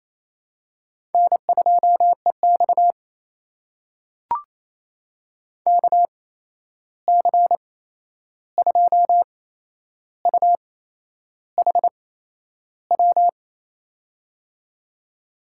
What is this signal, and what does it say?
Morse code, heard as a keyed tone, about 28 wpm, spaced as for 11 wpm, 700 Hz. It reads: N2EX KC2UHW
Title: USA Call Signs - 8x Character Spacing - 28wpm